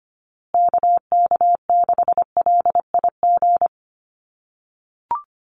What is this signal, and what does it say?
Morse code, heard as a keyed tone, about 25 wpm, 700 Hz. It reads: KK6LIG
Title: USA Call Signs Repeated 3x 25wpm